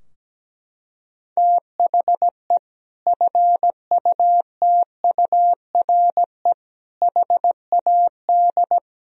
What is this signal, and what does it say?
Morse code, heard as a keyed tone, about 17 wpm, 700 Hz. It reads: THE FUTURE HAD
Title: Sentences from Top 1000 Words - Review Part 4 - 17wpm